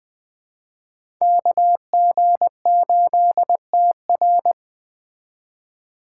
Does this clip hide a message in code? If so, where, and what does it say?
Yes: Morse code from 1.2 to 4.5 s, heard as a keyed tone, about 20 wpm, 700 Hz. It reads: KG8TR